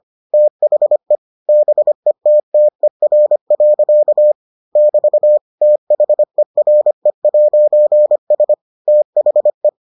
Morse code, heard as a keyed tone, about 25 wpm, 600 Hz. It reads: THE BETTER. <BT> THERE'S THE